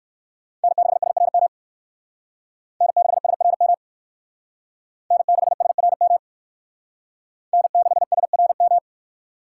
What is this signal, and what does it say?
Morse code, heard as a keyed tone, about 56 wpm, 700 Hz. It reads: N6SRM N6SRM N6SRM N6SRM